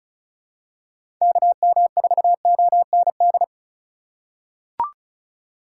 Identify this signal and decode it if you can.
Morse code, heard as a keyed tone, about 36 wpm, 700 Hz. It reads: KM4OND
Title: USA Call Signs - 6 Characters - 3x Character Spacing - 35wpm